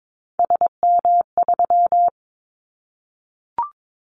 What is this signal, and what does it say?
Morse code, heard as a keyed tone, about 22 wpm, 700 Hz. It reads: SM3
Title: Callsign Prefixes for Top 25 DXCC - 22wpm